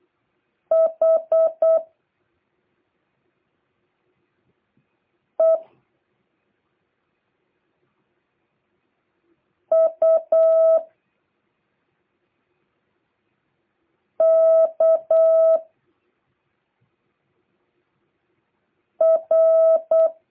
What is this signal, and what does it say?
Morse code, heard as a keyed tone, about 8 wpm, 650 Hz. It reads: H E U K R